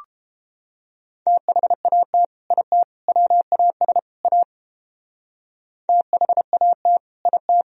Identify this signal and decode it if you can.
Morse code, heard as a keyed tone, about 29 wpm, 700 Hz. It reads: THAT IT WAS A THAT IT